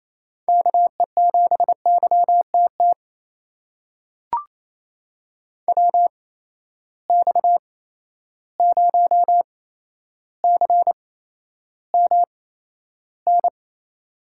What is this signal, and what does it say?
Morse code, heard as a keyed tone, about 28 wpm, spaced as for 12 wpm, 700 Hz. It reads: KE7YTT WX0CMN
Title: USA Call Signs - 6 Characters - 8x Character Spacing - 28wpm